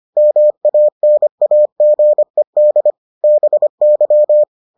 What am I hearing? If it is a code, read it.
Morse code, heard as a keyed tone, about 25 wpm, 600 Hz. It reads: MANAGED BY